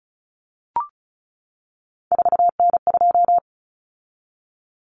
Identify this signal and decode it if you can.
Morse code, heard as a keyed tone, about 34 wpm, 700 Hz. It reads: 4N2
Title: Common Cut Numbers - 3-Digit 35wpm